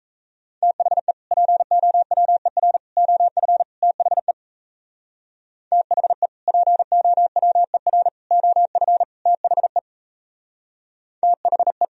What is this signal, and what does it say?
Morse code, heard as a keyed tone, about 38 wpm, 700 Hz. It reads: THE POWER OF THE THE POWER OF THE THE